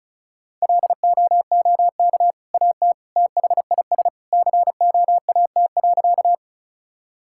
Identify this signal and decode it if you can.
Morse code, heard as a keyed tone, about 34 wpm, 700 Hz. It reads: LOOK AT THIS COAT.
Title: Sentences from Top 1600 Words 35wpm